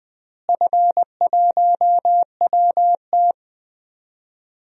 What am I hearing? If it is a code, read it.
Morse code, heard as a keyed tone, about 20 wpm, 700 Hz. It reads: F1WT
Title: DXCC-100 Call Signs - 4 Characters - Repeated 3x - 20wpm